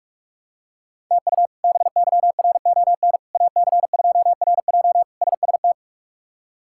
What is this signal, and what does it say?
Morse code, heard as a keyed tone, about 46 wpm, 700 Hz. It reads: TU BYRON AC2RJ SST
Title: K1USN SST QSO Practice - 45wpm